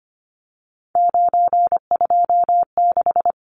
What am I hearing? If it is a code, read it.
Morse code, heard as a keyed tone, about 25 wpm, 690 Hz. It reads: 926
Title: Numbers - 3-Digit 25wpm